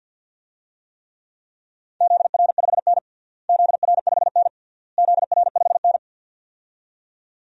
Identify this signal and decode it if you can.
Morse code, heard as a keyed tone, about 53 wpm, 700 Hz. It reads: ZR5N ZR5N ZR5N
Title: DXCC-100 Call Signs - 4 Characters - Repeated 3x - 50wpm